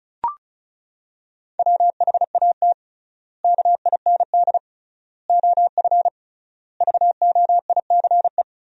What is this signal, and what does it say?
Morse code, heard as a keyed tone, about 34 wpm, 700 Hz. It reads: WHAT KIND OF VOICE
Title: 3x Word-Spacing - Sentences from Top 500 Words - 35wpm